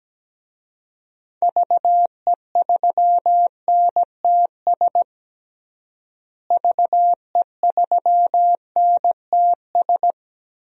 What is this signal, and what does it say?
Morse code, heard as a keyed tone, about 17 wpm, 700 Hz. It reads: VE3NTS VE3NTS
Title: DXCC-100 Call Signs - 6 Characters - Repeated 3x - 17wpm